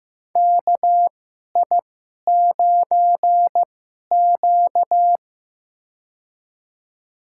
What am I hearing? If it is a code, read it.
Morse code, heard as a keyed tone, about 15 wpm, spaced as for 11 wpm, 700 Hz. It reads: KI9Q
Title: USA Call Signs - 4 Characters - 2x Character Spacing - 15wpm